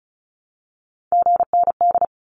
Morse code, heard as a keyed tone, about 34 wpm, 700 Hz. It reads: GND
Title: Abbreviations Repeated 3x - 35wpm